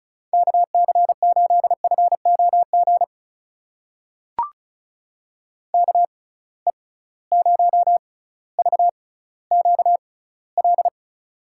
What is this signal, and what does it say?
Morse code, heard as a keyed tone, about 36 wpm, spaced as for 18 wpm, 690 Hz. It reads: KC8FOG KE0VQL
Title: USA Call Signs - 6 Characters - 6x Character Spacing - 35wpm